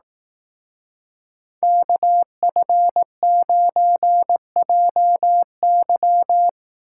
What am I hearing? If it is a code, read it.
Morse code, heard as a keyed tone, about 18 wpm, 700 Hz. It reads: KF9JY
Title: Speed Racing - USA 2x2 Call Signs 15wpm